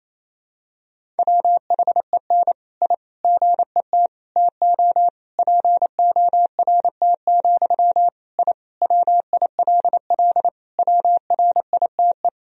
Morse code, heard as a keyed tone, about 28 wpm, 700 Hz. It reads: WHEN I GET TO PORT, I WILL WRITE
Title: Sentences from Top 1000 Words - Review Part 1 - 28wpm